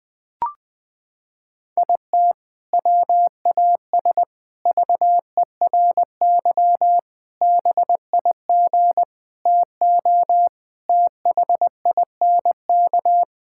Morse code, heard as a keyed tone, about 20 wpm, 700 Hz. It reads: IT WAS VERY BIG TO THINK
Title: Sentences from Top 1600 Words 20wpm